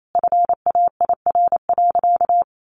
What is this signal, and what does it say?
Morse code, heard as a keyed tone, about 28 wpm, 700 Hz. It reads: FAIR.
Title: Sentences from Top 1300 Words 28wpm